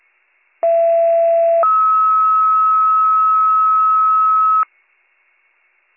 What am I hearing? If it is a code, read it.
Two-tone page, heard as a one-second tone, then a three-second tone, about 670 then 1290 Hz. It is heard from 0.6 to 4.6 s.